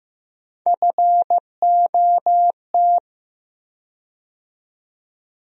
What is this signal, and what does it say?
Morse code, heard as a keyed tone, about 15 wpm, 700 Hz. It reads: FOT
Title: USA Call Sign Postfixes - 3 Letters 15wpm